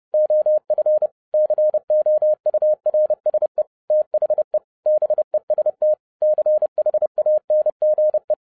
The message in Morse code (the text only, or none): OF COURSE THE BEST CHANGE